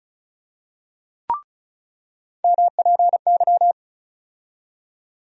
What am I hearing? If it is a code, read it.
Morse code, heard as a keyed tone, about 36 wpm, 700 Hz. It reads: MPY
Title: USA Call Sign Postfixes - 3 Letters 35wpm